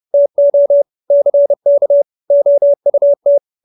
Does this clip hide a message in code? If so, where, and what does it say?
Yes: Morse code from 0.1 to 3.4 s, heard as a keyed tone, about 30 wpm, 570 Hz. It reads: TO CK OUT